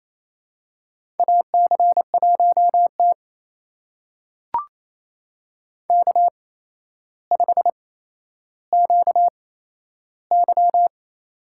Morse code, heard as a keyed tone, about 27 wpm, spaced as for 12 wpm, 700 Hz. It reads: AC1T K5QY
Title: USA Call Signs - 4 Characters - 8x Character Spacing - 28wpm